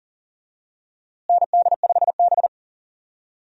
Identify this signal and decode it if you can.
Morse code, heard as a keyed tone, about 40 wpm, 700 Hz. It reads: ND5B
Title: USA Call Signs - 4 Characters - Repeated 3x - 40wpm